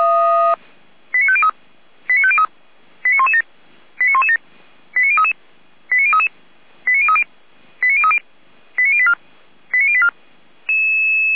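Five-tone selective call (ZVEI), heard as quick runs of five tones, tones of about 70 ms. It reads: 89503 89503 89117 89117 89033 89033 89030 89030 89084 89084
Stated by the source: ZVEI five-tone call received by SDR.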